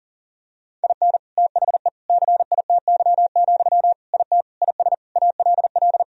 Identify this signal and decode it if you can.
Morse code, heard as a keyed tone, about 37 wpm, 700 Hz. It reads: IN THE CITY, IT IS ALL
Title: Sentences from Top 1000 Words - Review Part 3 - 40wpm